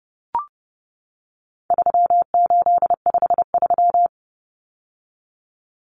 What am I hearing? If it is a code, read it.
Morse code, heard as a keyed tone, about 30 wpm, 700 Hz. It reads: 3853
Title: Common Cut Numbers - 4-Digit 30wpm